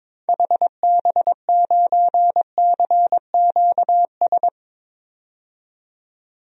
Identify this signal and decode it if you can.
Morse code, heard as a keyed tone, about 22 wpm, 700 Hz. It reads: HB9CQS
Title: DXCC-100 Call Signs - 6 Characters - 22wpm